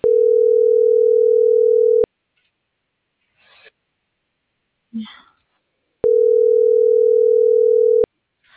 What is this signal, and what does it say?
Ringback tone (North America), 440 + 480 Hz.